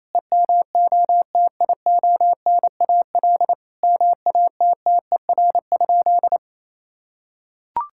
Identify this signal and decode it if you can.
Morse code, heard as a keyed tone, about 27 wpm, 700 Hz. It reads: EMOTIONAL MATTER?
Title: Sentences from Top 1000 Words - Review Part 5 - 28wpm